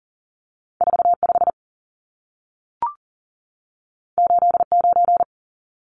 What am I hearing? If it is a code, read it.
Morse code, heard as a keyed tone, about 40 wpm, 700 Hz. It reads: 45 89
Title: Numbers - 2-Digit 40wpm